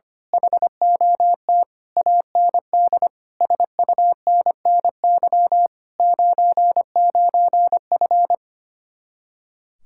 Morse code, heard as a keyed tone, about 24 wpm, 700 Hz. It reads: HOT AND SUNNY 99F
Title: Speed-Racing Common QSO Phrases - 25wpm